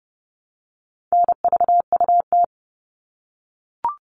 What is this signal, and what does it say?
Morse code, heard as a keyed tone, about 30 wpm, 700 Hz. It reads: NVUT
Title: All Cut Numbers - 4-Digit 30wpm